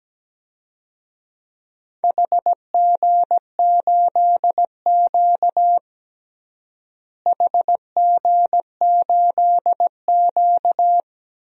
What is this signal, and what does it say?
Morse code, heard as a keyed tone, about 17 wpm, 700 Hz. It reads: HG8Q HG8Q
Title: DXCC-100 Call Signs - 4 Characters - Repeated 2x - 17wpm